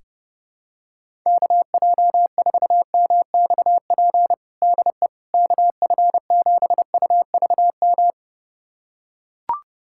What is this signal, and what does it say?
Morse code, heard as a keyed tone, about 30 wpm, 700 Hz. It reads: KJ4MXP DE KF7UVM
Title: USA Callsign DE Callsign 30wpm